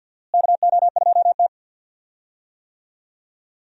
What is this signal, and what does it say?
Morse code, heard as a keyed tone, about 49 wpm, 700 Hz. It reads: KO2T
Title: USA Call Signs - 4 Characters - Repeated 3x - 50wpm